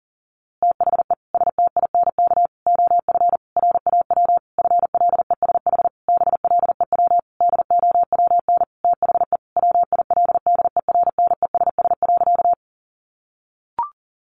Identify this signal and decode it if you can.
Morse code, heard as a keyed tone, about 40 wpm, 700 Hz. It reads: THE STINK OF RAW FLESH BLEW DOWN THE WILDERNESS.
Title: A Sound Of Thunder 40wpm